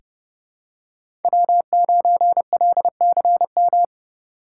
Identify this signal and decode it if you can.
Morse code, heard as a keyed tone, about 31 wpm, 700 Hz. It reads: W9LCM
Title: USA Call Signs - 5 Characters - 2x Character Spacing - 30wpm